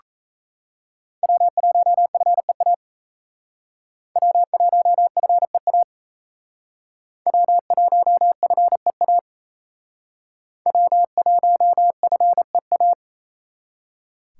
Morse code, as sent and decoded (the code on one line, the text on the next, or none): .-- .---- ..-. .- / .-- .---- ..-. .- / .-- .---- ..-. .- / .-- .---- ..-. .-
W1FEA W1FEA W1FEA W1FEA